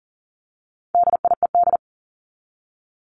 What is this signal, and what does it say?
Morse code, heard as a keyed tone, about 40 wpm, 700 Hz. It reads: DIED
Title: ED Suffix - Top 100 Words with Base Words 40wpm